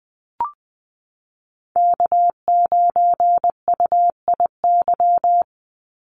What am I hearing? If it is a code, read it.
Morse code, heard as a keyed tone, about 20 wpm, 700 Hz. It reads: K9UIY
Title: Call Signs Repeated 2x - 20wpm